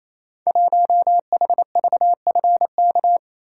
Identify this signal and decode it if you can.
Morse code, heard as a keyed tone, about 29 wpm, 700 Hz. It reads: 1HVFK
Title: Code Groups - 5 Letter:Numbers - 28wpm